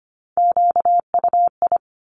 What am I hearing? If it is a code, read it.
Morse code, heard as a keyed tone, about 25 wpm, 700 Hz. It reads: QUI
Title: Top Three Letter Combinations in Words 25wpm